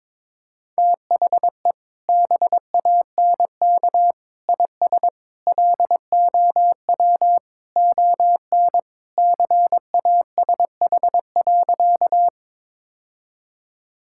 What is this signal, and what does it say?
Morse code, heard as a keyed tone, about 22 wpm, 700 Hz. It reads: THE BANK IS LOW ON CASH.
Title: Sentences from Top 1000 Words - Review Part 5 - 22wpm